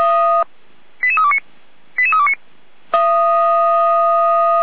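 Five-tone selective call (ZVEI), heard as quick runs of five tones, tones of about 70 ms. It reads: 88329 88329
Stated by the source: ZVEI five-tone selective call 88241 received by SDR.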